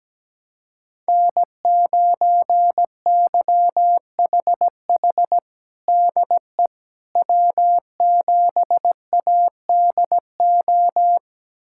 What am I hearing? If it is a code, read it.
Morse code, heard as a keyed tone, about 17 wpm, 700 Hz. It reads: N9YHH DE W7ADO
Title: USA Callsign DE Callsign - 5 Characters - 17wpm